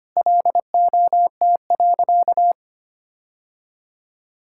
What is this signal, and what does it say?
Morse code, heard as a keyed tone, about 25 wpm, 700 Hz. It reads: LOT.
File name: practice / 25wpm / Sentences with 3-Word Phrases - 25wpm